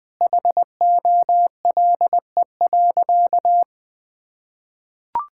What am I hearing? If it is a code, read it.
Morse code, heard as a keyed tone, about 21 wpm, 700 Hz. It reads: HOLE.
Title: Sentences from Top 1600 Words 20wpm